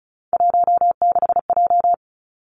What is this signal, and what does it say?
Morse code, heard as a keyed tone, about 34 wpm, 700 Hz. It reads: 16J
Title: Three Challenging Letter-Number Combinations 35wpm